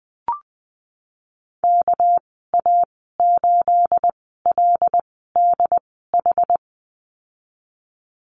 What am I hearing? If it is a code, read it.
Morse code, heard as a keyed tone, about 20 wpm, spaced as for 15 wpm, 700 Hz. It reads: KA8LDH
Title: 231 Lesson 37g - Letter x - Call Signs 20wpm